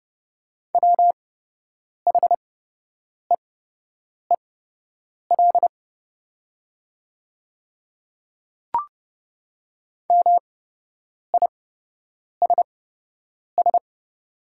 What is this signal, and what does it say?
Morse code, heard as a keyed tone, about 29 wpm, spaced as for 8 wpm, 700 Hz. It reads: WHEEL MISS
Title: 8x Character Spacing - Top 500 Words - 30wpm